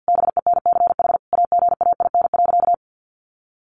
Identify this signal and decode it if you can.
Morse code, heard as a keyed tone, about 50 wpm, 700 Hz. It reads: BENCH AGAIN.
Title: Sentences from Top 1400 Words 50wpm